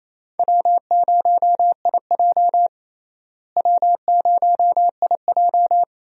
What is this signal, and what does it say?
Morse code, heard as a keyed tone, about 28 wpm, 700 Hz. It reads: W0IJ W0IJ
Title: USA Call Signs - 4 Characters - Repeated 2x - 28wpm